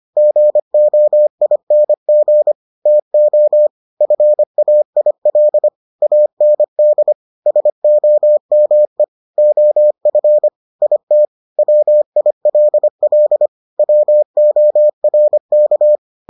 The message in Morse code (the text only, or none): GOING TO FAIL AND SOME OF IT WILL WORK